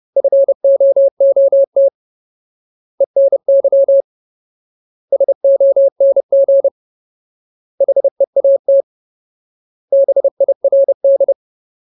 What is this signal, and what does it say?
Morse code, heard as a keyed tone, about 30 wpm, 550 Hz. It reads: FOOT ENY SONG HEAT BIRD
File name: Ultimate Word List / 30WPM Ultimate